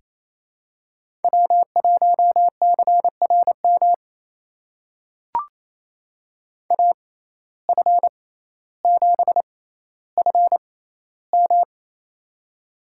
Morse code, heard as a keyed tone, about 28 wpm, spaced as for 14 wpm, 700 Hz. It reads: W1CRM AF7FM
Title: USA Call Signs - 5 Characters - 6x Character Spacing - 28wpm